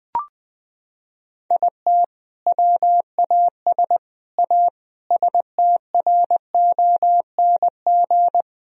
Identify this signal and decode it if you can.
Morse code, heard as a keyed tone, about 20 wpm, 700 Hz. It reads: IT WAS A STRONG
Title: Sentences from Top 1000 Words - Review Part 2 - 20wpm